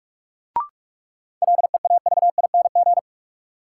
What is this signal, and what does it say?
Morse code, heard as a keyed tone, about 44 wpm, 700 Hz. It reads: LEAVING
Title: Sentences from Top 1000 Words - Review Part 4 - New Word Drill - 45wpm